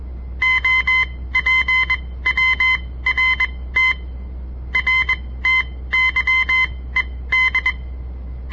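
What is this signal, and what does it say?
Morse code, heard as a keyed tone, about 21 wpm, spaced as for 16 wpm, 2050 Hz. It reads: OPWRT RTYED
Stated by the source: On-air Morse code practice broadcast by a southern Montana amateur radio association; all code, no read back.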